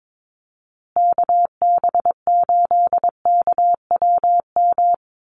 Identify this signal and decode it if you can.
Morse code, heard as a keyed tone, about 22 wpm, 700 Hz. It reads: KB8KWM
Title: USA Call Signs Repeated 3x 22wpm